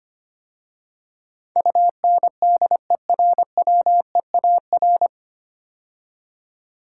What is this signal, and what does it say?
Morse code, heard as a keyed tone, about 25 wpm, 700 Hz. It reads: UNDERWEAR
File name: practice / 25wpm / 9 Letter Words - First 3 Letters Given - 25wpm